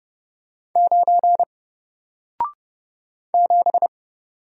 Morse code, heard as a keyed tone, about 30 wpm, 700 Hz. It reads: 9 7
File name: Single-Letter-Number-30wpm